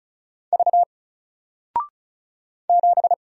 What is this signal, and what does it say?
Morse code, heard as a keyed tone, about 35 wpm, 700 Hz. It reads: V 7
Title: Single Letter-Number Alt 35wpm